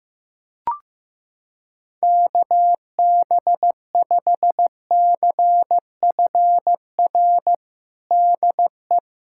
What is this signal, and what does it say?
Morse code, heard as a keyed tone, about 15 wpm, 700 Hz. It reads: KB5CFR DE
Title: USA Callsign DE Callsign 15wpm